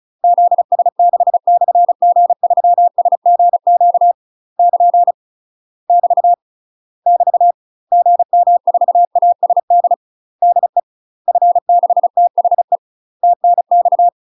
Morse code, heard as a keyed tone, about 36 wpm, 700 Hz. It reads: ZS6/G3SGQ <KN> <BT> <BT> GM4ASD DE F6THE TNX
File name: UK MASTER QSOs TRAINING / 35WPM / UK MASTER QSOs 0017